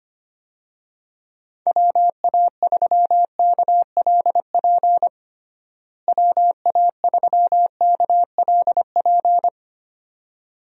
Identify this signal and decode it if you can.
Morse code, heard as a keyed tone, about 25 wpm, 700 Hz. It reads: WA3KLP WA3KLP
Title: USA Call Signs - 6 Characters - Repeated 2x - 25wpm